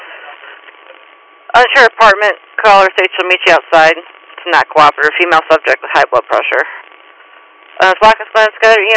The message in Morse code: none

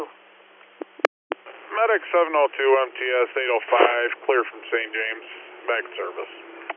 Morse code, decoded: none

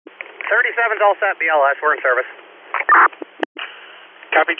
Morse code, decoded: none

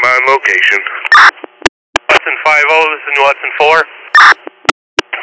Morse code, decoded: none

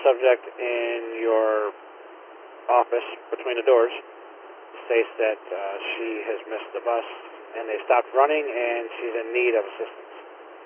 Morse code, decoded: none

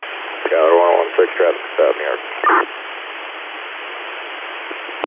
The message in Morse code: none

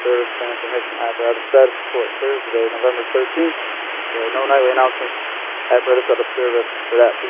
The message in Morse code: none

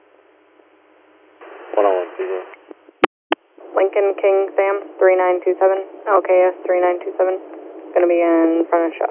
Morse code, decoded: none

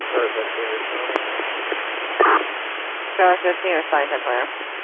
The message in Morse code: none

